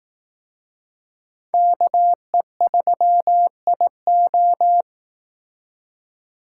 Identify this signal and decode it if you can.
Morse code, heard as a keyed tone, about 18 wpm, 700 Hz. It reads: KE3IO
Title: Speed Racing - USA Call Signs 15wpm